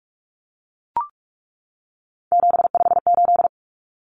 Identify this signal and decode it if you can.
Morse code, heard as a keyed tone, about 44 wpm, 700 Hz. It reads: 758
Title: Common Cut Numbers - 3-Digit 45wpm